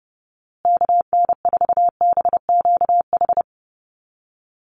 Morse code, heard as a keyed tone, about 30 wpm, 700 Hz. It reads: KN4BQH
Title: USA Call Signs Repeated 2x 30wpm